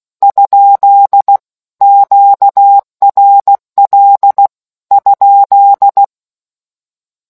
Morse code, heard as a keyed tone, about 16 wpm, 800 Hz. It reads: ? QRL ?